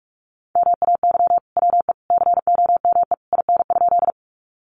Morse code, heard as a keyed tone, about 44 wpm, 700 Hz. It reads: MAY WE COME IN?
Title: Sentences from Top 100 Words 45wpm